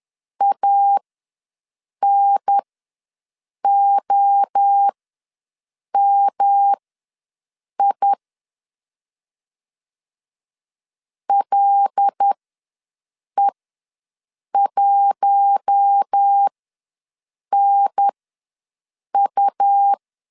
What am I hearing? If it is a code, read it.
Morse code, heard as a keyed tone, about 10 wpm, spaced as for 6 wpm, 800 Hz. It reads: ANOMI LE1NU
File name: LEZIONI / 5 LBVC / 5 L ws 10wpm 3636